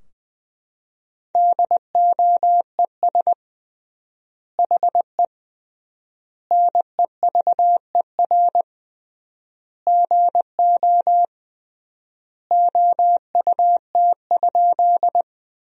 Morse code, heard as a keyed tone, about 20 wpm, 700 Hz. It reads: DOES HE NEVER GO OUT?
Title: 3x Word-Spacing - Sentences from Top 300 Words - 20wpm